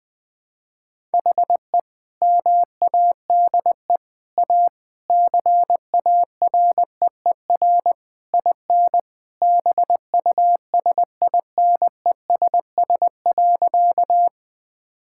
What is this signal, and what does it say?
Morse code, heard as a keyed tone, about 21 wpm, 700 Hz. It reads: HE MADE A CAREER IN BUSINESS.
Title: Sentences from Top 1000 Words - Review Part 6 - 20wpm